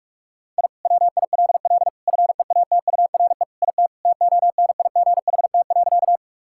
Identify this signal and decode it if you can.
Morse code, heard as a keyed tone, about 46 wpm, 700 Hz. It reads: I WILL FEATURE IT TONIGHT.
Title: Sentences from Top 1000 Words - Review Part 5 - 45wpm